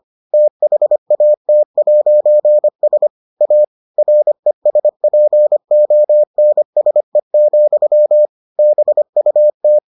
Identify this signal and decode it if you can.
Morse code, heard as a keyed tone, about 25 wpm, 600 Hz. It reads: THAT'S A RESPONSE, BUT